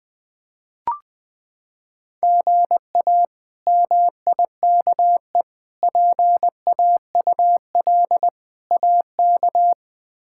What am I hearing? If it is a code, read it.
Morse code, heard as a keyed tone, about 20 wpm, 700 Hz. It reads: GA MIKE PAUL AK
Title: K1USN SST QSO Practice - 20wpm